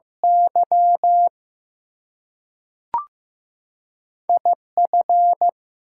Morse code, heard as a keyed tone, about 15 wpm, 700 Hz. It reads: Y IF
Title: Two Letter Words - 15wpm